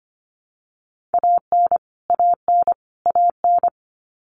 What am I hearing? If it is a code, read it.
Morse code, heard as a keyed tone, about 25 wpm, 700 Hz. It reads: AN AN AN
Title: Top 500 Words Repeated 25wpm